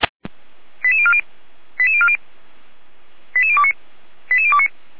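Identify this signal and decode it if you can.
Five-tone selective call (ZVEI), heard as quick runs of five tones, tones of about 70 ms. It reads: 88040 88040 88029 88029
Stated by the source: ZVEI five-tone call received by SDR, from the feuerwehr-orlamuende.org fire brigade server.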